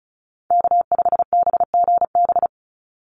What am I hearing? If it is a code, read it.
Morse code, heard as a keyed tone, about 34 wpm, 700 Hz. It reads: K5BGB
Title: Call Signs - 35wpm